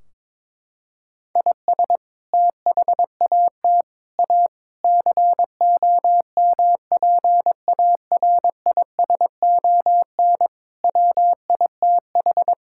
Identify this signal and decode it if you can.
Morse code, heard as a keyed tone, about 22 wpm, 700 Hz. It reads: IS THAT A COMPARISON WITH
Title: Sentences from Top 1600 Words 22wpm